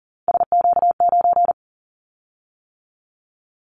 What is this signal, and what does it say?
Morse code, heard as a keyed tone, about 40 wpm, 700 Hz. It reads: SQ9
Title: Callsign Prefixes for Top 25 DXCC - 40wpm